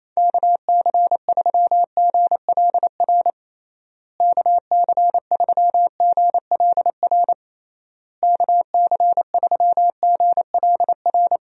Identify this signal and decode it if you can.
Morse code, heard as a keyed tone, about 28 wpm, 700 Hz. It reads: KC3GLR KC3GLR KC3GLR
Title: USA Call Signs - 6 Characters - Repeated 3x - 28wpm